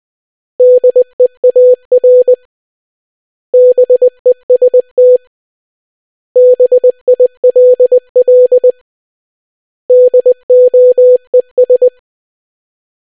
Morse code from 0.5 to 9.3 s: DEAR BEST BILL